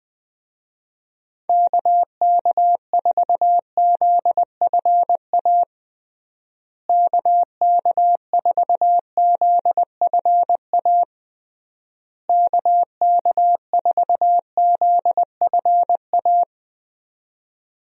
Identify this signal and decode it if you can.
Morse code, heard as a keyed tone, about 20 wpm, 700 Hz. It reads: KK4ZFA KK4ZFA KK4ZFA
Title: USA Call Signs - 6 Characters - Repeated 3x - 20wpm